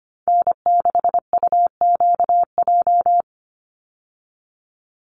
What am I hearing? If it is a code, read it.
Morse code, heard as a keyed tone, about 25 wpm, 700 Hz. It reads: N6UQJ
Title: USA Call Signs 25wpm